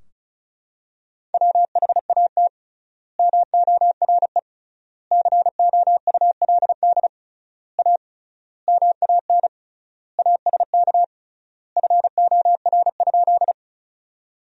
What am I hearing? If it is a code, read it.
Morse code, heard as a keyed tone, about 34 wpm, 700 Hz. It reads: WHAT MORE COULD A MAN ASK FOR?
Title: 3x Word-Spacing - Sentences from Top 200 Words - 35wpm